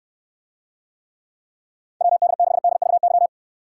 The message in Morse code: WD6DVY

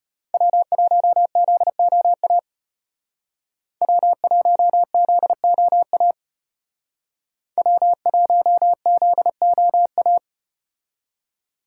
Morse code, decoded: W1ZOA W1ZOA W1ZOA

